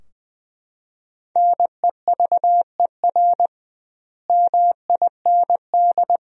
Morse code, heard as a keyed tone, about 20 wpm, 700 Hz. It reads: NEVER MIND
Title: 2x Word-Spacing - Sentences from Top 500 Words - 20wpm